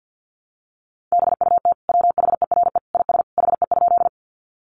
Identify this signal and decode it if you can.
Morse code, heard as a keyed tone, about 50 wpm, 700 Hz. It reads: BUT WHERE IS HE?
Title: Sentences from Top 200 Words 50wpm